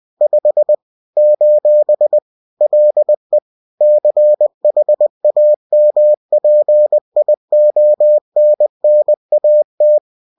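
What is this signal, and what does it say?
Morse code, heard as a keyed tone, about 20 wpm, 600 Hz. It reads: 5 : LE CHAMPIONNAT
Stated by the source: Bulletin sent in CW.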